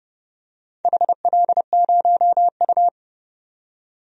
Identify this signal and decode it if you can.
Morse code, heard as a keyed tone, about 30 wpm, 700 Hz. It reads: HL0U
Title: DXCC-100 Call Signs - 4 Characters - 30wpm